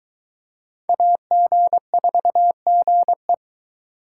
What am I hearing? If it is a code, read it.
Morse code, heard as a keyed tone, about 23 wpm, 700 Hz. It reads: AG4GE